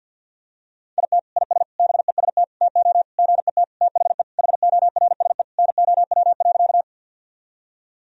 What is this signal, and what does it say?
Morse code, heard as a keyed tone, about 51 wpm, 700 Hz. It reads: IT IS BEST TO GET THE HORSE NOW.
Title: Sentences from Top 1000 Words - Review Part 1 - 50wpm